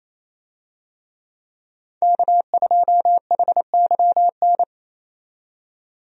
Morse code, heard as a keyed tone, about 28 wpm, 700 Hz. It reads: K2HYN
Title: USA Call Signs - 5 Characters - 28wpm